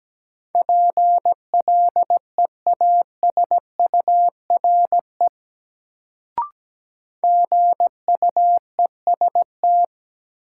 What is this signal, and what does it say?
Morse code, heard as a keyed tone, about 17 wpm, 700 Hz. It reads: PLEASURE GUEST
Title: Sentences from Top 1600 Words - New Word Drill - 17wpm